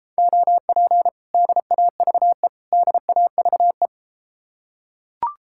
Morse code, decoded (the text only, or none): OP DAVE DAVE